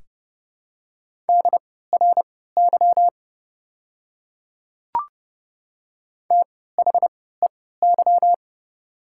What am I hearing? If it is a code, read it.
Morse code, heard as a keyed tone, about 29 wpm, spaced as for 17 wpm, 700 Hz. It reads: DRY THEY
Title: 3x Character Spacing - Top 500 Words - 30wpm